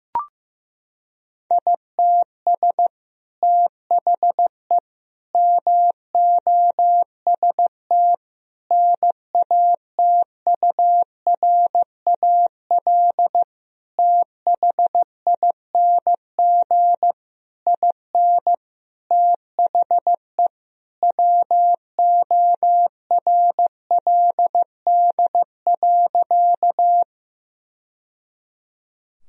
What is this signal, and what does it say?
Morse code, heard as a keyed tone, about 16 wpm, 700 Hz. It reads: ITS THE MOST NATURAL THING IN THE WORLD.
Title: Sentences from Top 1000 Words - Review Part 4 - 15wpm